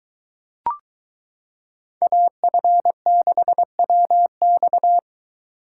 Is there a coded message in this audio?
Yes: Morse code from 2.0 to 5.0 s, heard as a keyed tone, about 23 wpm, 700 Hz. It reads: AF6WX